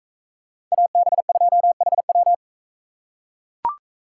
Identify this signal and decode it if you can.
Morse code, heard as a keyed tone, about 42 wpm, 700 Hz. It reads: AB2HW